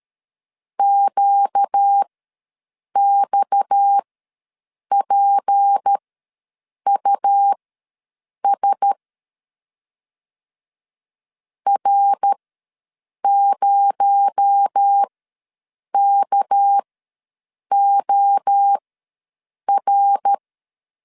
Morse code, heard as a keyed tone, about 13 wpm, spaced as for 7 wpm, 800 Hz. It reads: QXPUS R0KOR
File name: LEZIONI / 6 PKX0 / 6 PKX0 ws 12wpm 3448